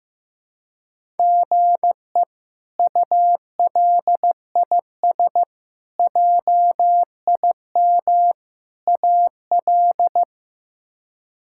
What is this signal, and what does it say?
Morse code, heard as a keyed tone, about 15 wpm, 700 Hz. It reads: GE ULIS JIM AL
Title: K1USN SST QSO Practice - 15wpm